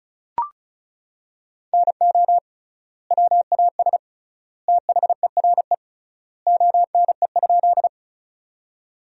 Morse code, heard as a keyed tone, about 36 wpm, 700 Hz. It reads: NO WAS THERE ONE?
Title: 3x Word-Spacing - Sentences from Top 100 Words - 35wpm